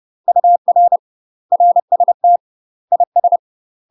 Morse code, heard as a keyed tone, about 30 wpm, 700 Hz. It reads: UR RST IS